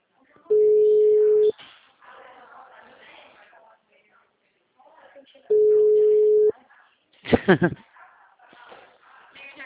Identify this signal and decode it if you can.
Ringback tone (Europe, 425 Hz), with speech.